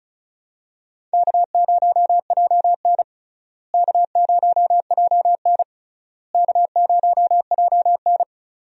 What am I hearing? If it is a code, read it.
Morse code, heard as a keyed tone, about 34 wpm, 700 Hz. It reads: K0JN K0JN K0JN
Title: USA Call Signs - 4 Characters - Repeated 3x - 35wpm